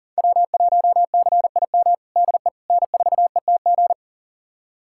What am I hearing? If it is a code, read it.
Morse code, heard as a keyed tone, about 40 wpm, 700 Hz. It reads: W1CIM DE N4ETG
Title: USA Callsign DE Callsign - 5 Characters - 40wpm